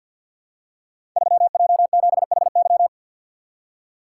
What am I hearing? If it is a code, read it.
Morse code, heard as a keyed tone, about 52 wpm, 700 Hz. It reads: 3J7HY
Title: Code Groups - 5 Letter:Numbers - 50wpm